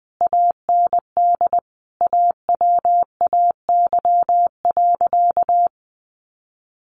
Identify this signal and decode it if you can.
Morse code, heard as a keyed tone, about 20 wpm, 700 Hz. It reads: AND AWAY.
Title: A Sound Of Thunder 20wpm